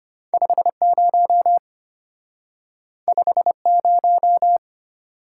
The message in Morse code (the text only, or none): 50 50